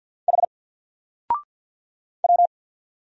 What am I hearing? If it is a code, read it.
Morse code, heard as a keyed tone, about 50 wpm, 700 Hz. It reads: H W
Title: Single Character (Letter:Number:Punctuation:Prosign) 50wpm